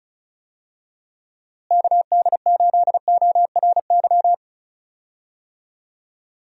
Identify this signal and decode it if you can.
Morse code, heard as a keyed tone, about 34 wpm, 700 Hz. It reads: KD8ORY
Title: USA Call Signs - 6 Characters - Repeated 2x - 35wpm